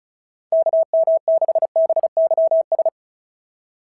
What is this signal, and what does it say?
Morse code, heard as a keyed tone, about 36 wpm, 690 Hz. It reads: KM6BYS
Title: SOTA QSO Practice 35wpm